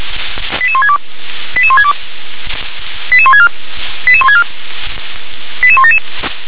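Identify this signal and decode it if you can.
Five-tone selective call (ZVEI), heard as quick runs of five tones, tones of about 70 ms. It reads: 88162 88162 88164 88164 88177
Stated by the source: ZVEI five-tone selective call received on an SDR.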